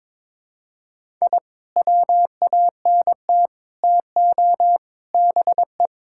Morse code, heard as a keyed tone, about 22 wpm, 700 Hz. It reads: I WANT TO BE
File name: practice / 22wpm / Sentences with 3-Word Phrases - 22wpm